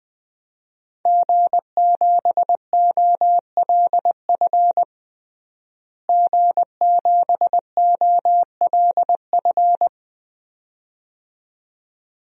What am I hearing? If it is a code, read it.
Morse code, heard as a keyed tone, about 20 wpm, 700 Hz. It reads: G7OLF G7OLF
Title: DXCC-100 Call Signs - 5 Characters - Repeated 2x - 20wpm